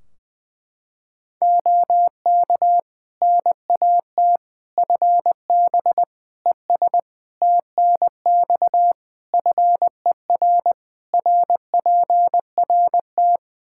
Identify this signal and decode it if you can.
Morse code, heard as a keyed tone, about 21 wpm, 700 Hz. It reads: OK NAT FB ES TNX FER RPRT
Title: Practice QSO - Exchange 2 - Info - 20wpm